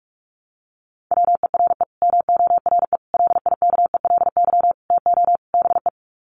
Morse code, heard as a keyed tone, about 44 wpm, 700 Hz. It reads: WERE MORE LIKELY TO BE
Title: Sets of 5 Words Encore2 45wpm